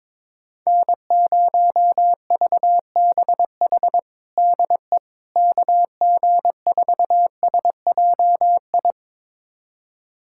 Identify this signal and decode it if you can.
Morse code, heard as a keyed tone, about 22 wpm, 700 Hz. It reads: N0VBH DE KG4SJI
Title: USA Callsign DE Callsign 22wpm